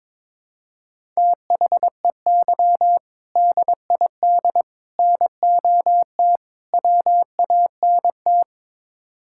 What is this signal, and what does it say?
Morse code, heard as a keyed tone, about 22 wpm, 700 Hz. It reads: THEY DID NOT WANT
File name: practice / 22wpm / Build A Phrase - 5-Word Phrases - 22wpm